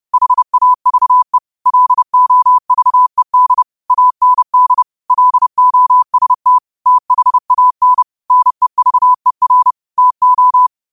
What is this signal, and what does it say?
Morse code, heard as a keyed tone, about 30 wpm, 1000 Hz. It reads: HAVE LOVED AND LOST THAN NEVER TO
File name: qotdfiles / 30wpm